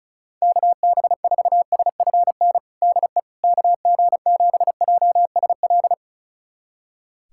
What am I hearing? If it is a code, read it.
Morse code, heard as a keyed tone, about 36 wpm, 700 Hz. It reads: KB4SFN DE KG7JSL
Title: USA Callsign DE Callsign - 6 Characters - 35wpm